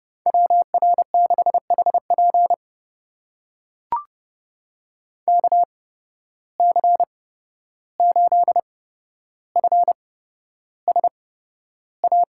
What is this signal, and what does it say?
Morse code, heard as a keyed tone, about 30 wpm, spaced as for 12 wpm, 700 Hz. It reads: WR6HP KC8FSA
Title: USA Call Signs - 8x Character Spacing - 30wpm